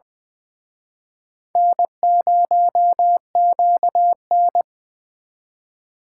Morse code, heard as a keyed tone, about 24 wpm, 700 Hz. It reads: N0QN
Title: Speed Racing - USA 1x2 Call Signs 20wpm